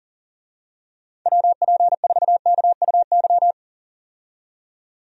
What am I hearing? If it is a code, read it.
Morse code, heard as a keyed tone, about 40 wpm, 700 Hz. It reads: WP4KUY
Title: USA Call Signs - 6 Characters - 40wpm